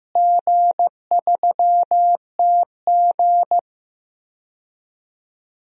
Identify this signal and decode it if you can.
Morse code, heard as a keyed tone, about 15 wpm, 700 Hz. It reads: G3TG
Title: DXCC-100 Call Signs - 4 Characters - Repeated 2x - 15wpm